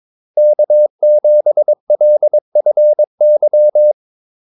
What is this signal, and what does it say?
Morse code, heard as a keyed tone, about 22 wpm, 600 Hz. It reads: K7LFY
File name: Callsigns-22wpm